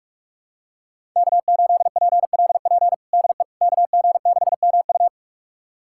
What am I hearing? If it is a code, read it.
Morse code, heard as a keyed tone, about 46 wpm, 700 Hz. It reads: K8PLP DE KG6MU
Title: USA Callsign DE Callsign - 5 Characters - 45wpm